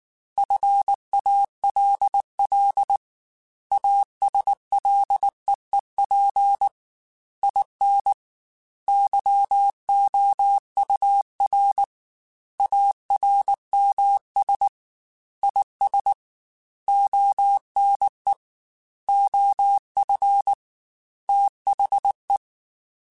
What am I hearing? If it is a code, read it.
Morse code, heard as a keyed tone, about 19 wpm, 800 Hz. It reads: FALL ASLEEP IN YOUR ARMS IS ONE OF THE
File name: Quotes / Quotes-mp3s / ChildAsleep @18